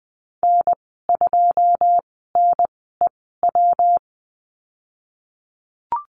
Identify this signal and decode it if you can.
Morse code, heard as a keyed tone, about 19 wpm, spaced as for 14 wpm, 700 Hz. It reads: N2NEW
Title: 122 Lesson 20g - Number 3 - Call Signs 20wpm